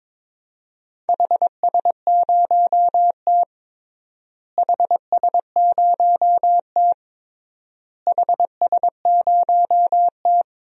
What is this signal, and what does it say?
Morse code, heard as a keyed tone, about 22 wpm, 700 Hz. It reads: HS0T HS0T HS0T
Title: DXCC-100 Call Signs - 4 Characters - Repeated 3x - 22wpm